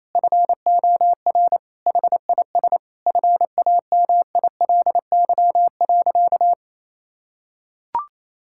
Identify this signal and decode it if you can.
Morse code, heard as a keyed tone, about 28 wpm, 700 Hz. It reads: FOR HIS FAMILY.
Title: Sentences from Top 1000 Words - Review Part 5 - 28wpm